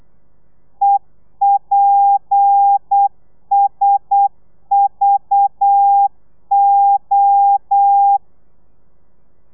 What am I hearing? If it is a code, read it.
Morse code, heard as a keyed tone, about 8 wpm, 800 Hz. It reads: EPSVO